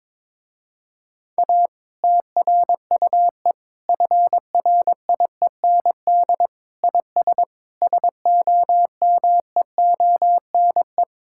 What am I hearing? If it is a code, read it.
Morse code, heard as a keyed tone, about 22 wpm, 700 Hz. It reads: A TRUE FRIEND IS SOMEONE
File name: Difficult Sentences with Hints - 22wpm